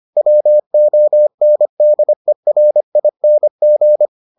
Morse code, heard as a keyed tone, about 25 wpm, 600 Hz. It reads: WONDERING